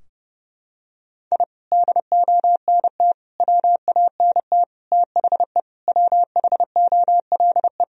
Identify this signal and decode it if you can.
Morse code, heard as a keyed tone, about 30 wpm, 700 Hz. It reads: I DONT WANT THE WHOLE
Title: Sentences from Top 1000 Words - Review Part 2 - 30wpm